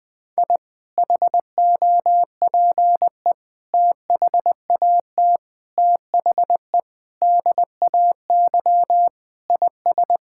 Morse code, heard as a keyed tone, about 20 wpm, 700 Hz. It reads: I HOPE THAT THE DAY IS
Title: Sentences with 2-Word Phrases - 20wpm